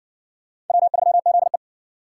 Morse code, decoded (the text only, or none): W3ZE